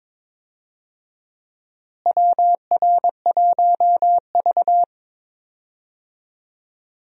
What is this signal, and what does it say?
Morse code, heard as a keyed tone, about 22 wpm, 700 Hz. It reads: WR1V
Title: USA Call Signs - 4 Characters - Repeated 2x - 22wpm